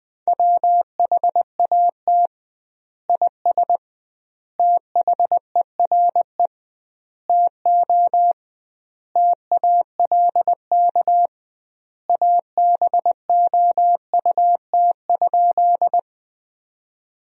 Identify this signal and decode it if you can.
Morse code, heard as a keyed tone, about 20 wpm, 700 Hz. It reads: WHAT IS THERE TO TALK ABOUT?
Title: 2x Word-Spacing - Sentences from Top 400 Words - 20wpm